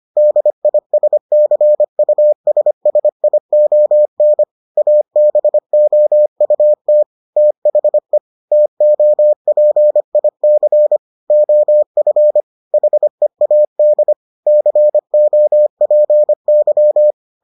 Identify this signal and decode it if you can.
Morse code, heard as a keyed tone, about 25 wpm, 600 Hz. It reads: DISCUSSION ABOUT THE TOPIC OF HEAD COPY